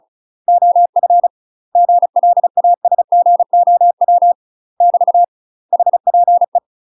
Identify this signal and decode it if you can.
Morse code, heard as a keyed tone, about 34 wpm, 700 Hz. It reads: OF GLASGOW <BT> HPE
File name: UK MASTER QSOs TRAINING / 35WPM / UK MASTER QSOs 0008